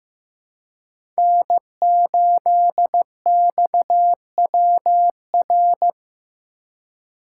Morse code, heard as a keyed tone, about 15 wpm, 700 Hz. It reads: N8XWR
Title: USA Call Signs - 5 Characters - 3x Character Spacing - 15wpm